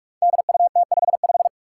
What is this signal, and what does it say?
Morse code, heard as a keyed tone, about 46 wpm, 700 Hz. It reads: DUT55
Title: Five Challenging Letter-Number Combinations 45wpm